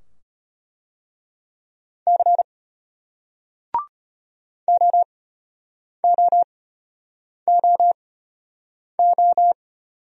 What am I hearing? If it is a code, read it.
Morse code, heard as a keyed tone, about 33 wpm, spaced as for 8 wpm, 700 Hz. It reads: C OOOO